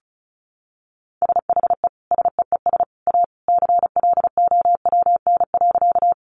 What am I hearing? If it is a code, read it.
Morse code, heard as a keyed tone, about 34 wpm, 700 Hz. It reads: SHE SEES A CLOWN.
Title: Easy Phrases 35wpm